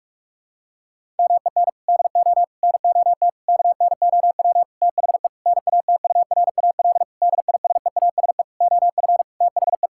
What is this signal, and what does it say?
Morse code, heard as a keyed tone, about 46 wpm, 700 Hz. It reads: MEN DO NOT KNOW THE NATURAL DISEASE OF THE
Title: Sentences from Top 1000 Words - Review Part 6 - 45wpm